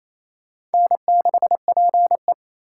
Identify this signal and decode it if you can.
Morse code, heard as a keyed tone, about 34 wpm, 700 Hz. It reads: N6PE
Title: Speed Racing - USA Call Signs 28wpm